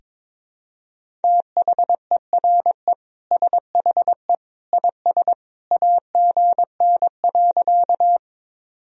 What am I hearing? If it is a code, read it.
Morse code, heard as a keyed tone, about 22 wpm, 700 Hz. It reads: THERE SHE IS AGN.
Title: Sentences from Top 200 Words and Abbreviations 22wpm